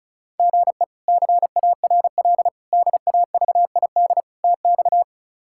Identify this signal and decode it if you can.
Morse code, heard as a keyed tone, about 36 wpm, 700 Hz. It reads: GE CARL DAVID TX
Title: K1USN SST QSO Practice - 35wpm